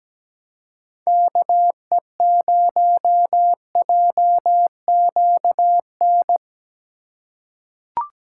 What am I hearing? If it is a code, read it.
Morse code, heard as a keyed tone, about 17 wpm, 700 Hz. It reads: KE0JQN
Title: USA Call Signs - 6 Characters - 3x Character Spacing - 17wpm